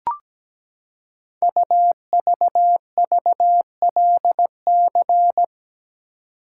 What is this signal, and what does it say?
Morse code, heard as a keyed tone, about 17 wpm, 700 Hz. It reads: UVVLC
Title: Code Groups - 5 Letters - 17wpm